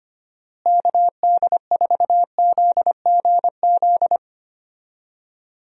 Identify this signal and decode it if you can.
Morse code, heard as a keyed tone, about 25 wpm, 700 Hz. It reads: KD4ZGZ
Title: USA Call Signs - 6 Characters - Repeated 3x - 25wpm